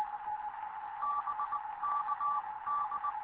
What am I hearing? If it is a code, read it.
Morse code, heard as a keyed tone, about 19 wpm, spaced as for 17 wpm, 1160 Hz, with radio noise. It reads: BK D
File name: Morze-HA2MN / hangp / 2 11 GYAKORLAT